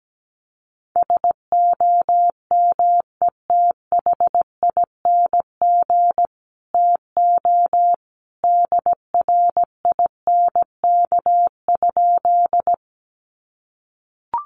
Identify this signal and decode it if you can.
Morse code, heard as a keyed tone, about 17 wpm, 700 Hz. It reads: SOMETHING TO DRINK?
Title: Sentences from Top 1500 Words 17wpm